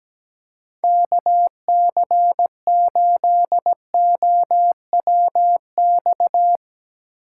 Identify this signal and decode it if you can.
Morse code, heard as a keyed tone, about 17 wpm, 700 Hz. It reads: KC8OWX
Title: Speed Racing - USA 2x3 Call Signs 17wpm